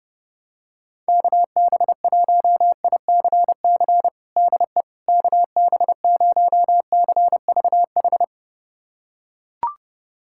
Morse code, heard as a keyed tone, about 30 wpm, 700 Hz. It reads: KB1ICC DE KB0CVH
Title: USA Callsign DE Callsign 30wpm